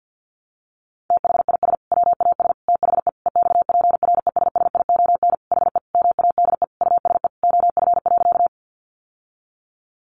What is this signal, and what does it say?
Morse code, heard as a keyed tone, about 50 wpm, 700 Hz. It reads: THIS WAS THE EXPRESSION HE MADE USE OF.
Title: Sentences from Top 1200 Words 50wpm